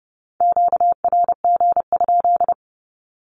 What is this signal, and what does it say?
Morse code, heard as a keyed tone, about 30 wpm, 700 Hz. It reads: QRG?
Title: Q-Codes 30wpm